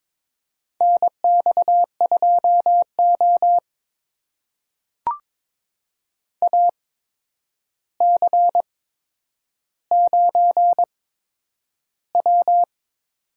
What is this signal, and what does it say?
Morse code, heard as a keyed tone, about 22 wpm, spaced as for 9 wpm, 700 Hz. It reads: NX2O AC9W
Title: USA Call Signs - 4 Characters - 8x Character Spacing - 22wpm